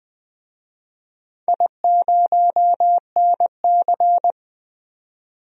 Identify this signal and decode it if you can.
Morse code, heard as a keyed tone, about 20 wpm, 700 Hz. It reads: I0NC
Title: DXCC-100 Call Signs - 4 Characters - 20wpm